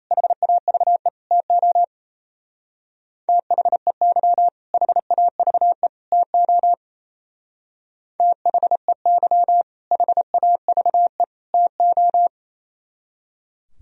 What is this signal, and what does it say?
Morse code, heard as a keyed tone, about 35 wpm, 700 Hz. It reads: HAVE TO THEY HAVE TO THEY HAVE TO